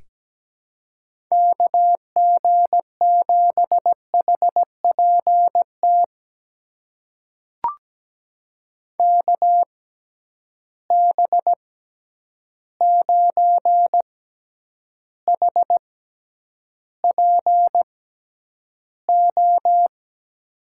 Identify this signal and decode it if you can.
Morse code, heard as a keyed tone, about 17 wpm, spaced as for 9 wpm, 700 Hz. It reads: KG7HPT KB9HPO
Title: USA Call Signs - 6x Character Spacing - 17wpm